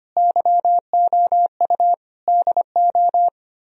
Morse code, heard as a keyed tone, about 33 wpm, 700 Hz. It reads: YOU DO